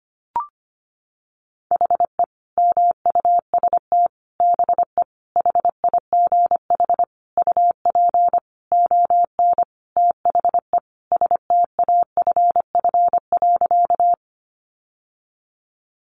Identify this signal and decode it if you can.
Morse code, heard as a keyed tone, about 25 wpm, 700 Hz. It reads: HE MUST BE HIGH UP ON THE STAFF.
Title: Sentences from Top 1200 Words 25wpm